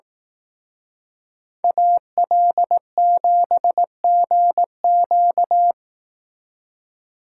Morse code, heard as a keyed tone, about 18 wpm, 700 Hz. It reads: AL7GQ
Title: Speed Racing - USA Call Signs 15wpm